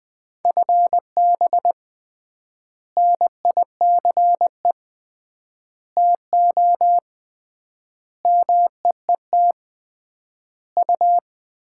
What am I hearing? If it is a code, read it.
Morse code, heard as a keyed tone, about 20 wpm, 700 Hz. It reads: FB NICE TO MEET U
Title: Common QSO Phrases - 3x Word-Spacing - 20wpm